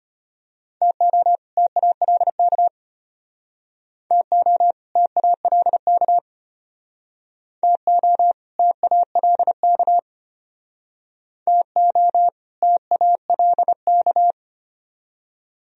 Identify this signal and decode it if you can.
Morse code, heard as a keyed tone, about 34 wpm, 700 Hz. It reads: TO TALK TO TALK TO TALK TO TALK